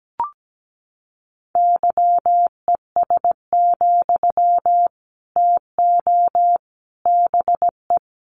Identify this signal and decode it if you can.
Morse code, heard as a keyed tone, about 17 wpm, 700 Hz. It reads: YES, TO BE
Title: Sentences from Top 500 Words 17wpm